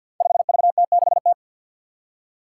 Morse code, heard as a keyed tone, about 50 wpm, 700 Hz. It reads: 5VT6T
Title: Five Challenging Letter-Number Combinations 50wpm